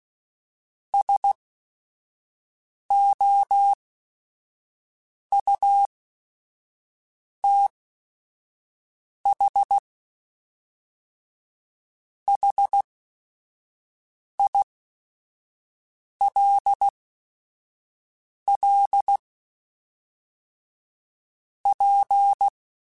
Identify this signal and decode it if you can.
Morse code, heard as a keyed tone, about 16 wpm, spaced as for 5 wpm, 800 Hz. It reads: SOUTH HILL P